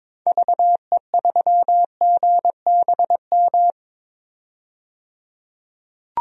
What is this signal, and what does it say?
Morse code, heard as a keyed tone, about 22 wpm, 700 Hz. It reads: VE3GBM
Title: DXCC-100 Call Signs - 6 Characters - 22wpm